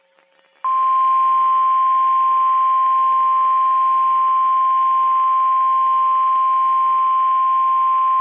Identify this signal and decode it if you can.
Emergency Alert System (SAME) tones (the 1050 Hz attention tone). It is heard from 1.0 to 7.4 s.
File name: eas messages / EAS WXM31 Elmira